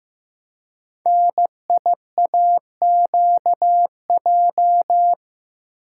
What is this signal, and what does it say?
Morse code, heard as a keyed tone, about 15 wpm, 700 Hz. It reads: NIAQJ
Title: Code Groups - 5 Letters - 15wpm